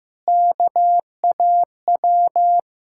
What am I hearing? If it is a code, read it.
Morse code, heard as a keyed tone, about 15 wpm, 690 Hz. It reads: KAW
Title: Ontario QSO Party Exchange 15wpm